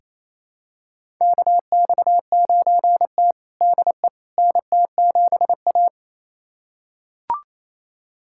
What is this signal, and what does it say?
Morse code, heard as a keyed tone, about 29 wpm, 700 Hz. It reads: KX9T DE NT7A
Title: USA Callsign DE Callsign - 4 Characters - 28wpm